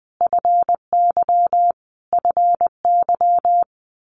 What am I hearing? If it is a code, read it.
Morse code, heard as a keyed tone, about 20 wpm, 700 Hz. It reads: FY FY
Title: Multi-letter Phonemes Repeated 3x 20wpm